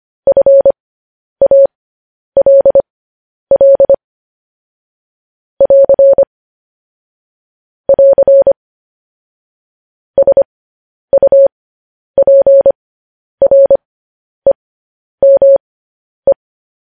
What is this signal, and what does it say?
Morse code, heard as a keyed tone, about 25 wpm, spaced as for 10 wpm, 550 Hz. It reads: FALL <AR> <AR> SUPREME
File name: podcast / News headlines-1x-25x10